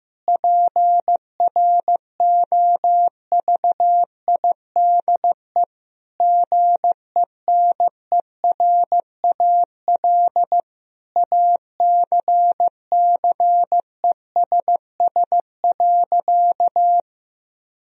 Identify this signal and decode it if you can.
Morse code, heard as a keyed tone, about 15 wpm, 700 Hz. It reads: PROVIDE GENERAL ACCESS.
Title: Sentences from Top 1000 Words - Review Part 4 - 15wpm